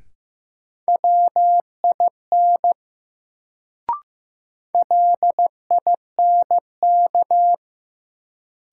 Morse code, heard as a keyed tone, about 15 wpm, 700 Hz. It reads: WIN LINK
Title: Sentences from Top 1500 Words - New Word Drill Alt - 15wpm